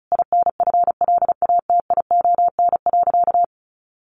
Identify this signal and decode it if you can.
Morse code, heard as a keyed tone, about 34 wpm, 700 Hz. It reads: INFLATION.
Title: Sentences from Top 1100 Words 35wpm